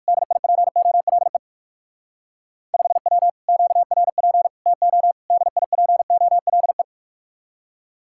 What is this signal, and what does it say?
Morse code, heard as a keyed tone, about 58 wpm, 700 Hz. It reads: DIPOLE 5W QRP TO DIPOLE